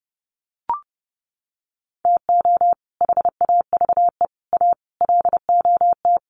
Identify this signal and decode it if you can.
Morse code, heard as a keyed tone, about 30 wpm, 700 Hz. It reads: TO HAVE A LOT
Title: Sets of 5 Words Encore 30wpm